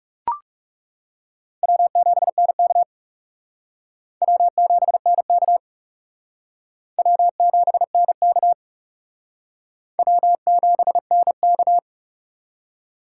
Morse code, heard as a keyed tone, about 36 wpm, 700 Hz. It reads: W7NK W7NK W7NK W7NK
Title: Speed Racing - USA Call Signs 30wpm